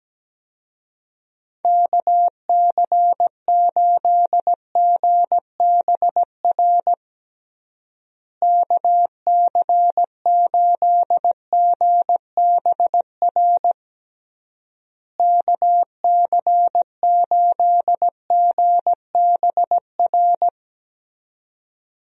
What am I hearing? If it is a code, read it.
Morse code, heard as a keyed tone, about 17 wpm, 700 Hz. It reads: KC8GBR KC8GBR KC8GBR
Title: USA Call Signs - 6 Characters - Repeated 3x - 17wpm